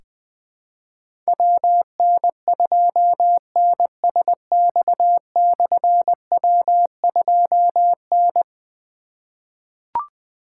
Morse code, heard as a keyed tone, about 20 wpm, 700 Hz. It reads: WN2NSX/W2N
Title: Difficult Call Signs 20wpm